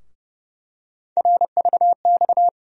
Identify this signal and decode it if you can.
Morse code, heard as a keyed tone, about 30 wpm, 700 Hz. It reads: RVX
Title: USA Call Sign Postfixes - 3 Letters 30wpm